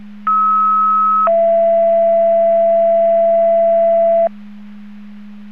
Two-tone page, heard as a one-second tone, then a three-second tone, about 1290 then 670 Hz. It is heard from 0.3 to 4.3 s.